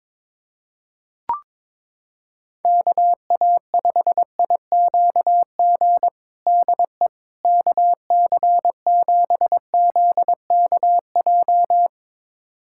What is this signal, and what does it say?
Morse code, heard as a keyed tone, about 22 wpm, 700 Hz. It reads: KA5IQG DE KC7ZKJ
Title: USA Callsign DE Callsign - 6 Characters - 22wpm